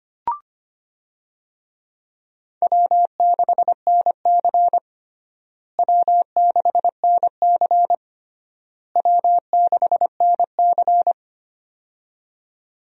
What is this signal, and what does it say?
Morse code, heard as a keyed tone, about 25 wpm, 700 Hz. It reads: W6NC W6NC W6NC
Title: USA Call Signs - 4 Characters - Repeated 3x - 25wpm